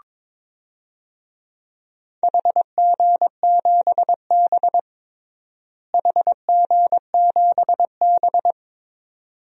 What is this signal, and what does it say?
Morse code, heard as a keyed tone, about 23 wpm, 700 Hz. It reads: HG7B HG7B
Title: DXCC-100 Call Signs - 4 Characters - Repeated 2x - 22wpm